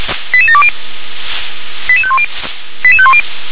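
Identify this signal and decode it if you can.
Five-tone selective call (ZVEI), heard as quick runs of five tones, tones of about 70 ms. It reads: 88720 88510 88510
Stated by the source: ZVEI five-tone selective call received on SDR.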